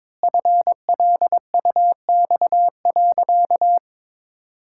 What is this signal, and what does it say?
Morse code, heard as a keyed tone, about 21 wpm, 700 Hz. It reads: FLUX.
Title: Pangram Challenge v2 - 22wpm